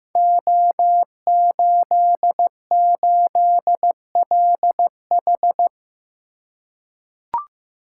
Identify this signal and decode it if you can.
Morse code, heard as a keyed tone, about 15 wpm, 700 Hz. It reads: O88LH
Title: Code Groups - 5 Letter:Numbers - 15wpm